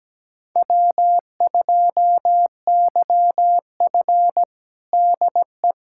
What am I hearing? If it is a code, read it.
Morse code, heard as a keyed tone, about 17 wpm, 700 Hz. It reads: W2YF DE
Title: USA Callsign DE Callsign - 4 Characters - 17wpm